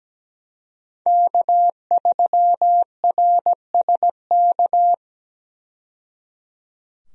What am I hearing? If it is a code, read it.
Morse code, heard as a keyed tone, about 17 wpm, 700 Hz. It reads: K3RSK
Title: USA Call Signs - 5 Characters - 17wpm